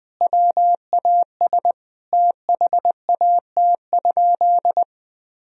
Morse code, heard as a keyed tone, about 20 wpm, 700 Hz. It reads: WAS THAT?
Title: Sentences from Top 1000 Words - Review Part 3 - 20wpm